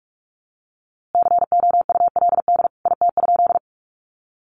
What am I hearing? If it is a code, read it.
Morse code, heard as a keyed tone, about 44 wpm, 700 Hz. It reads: COULD IT?
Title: A Sound Of Thunder 45wpm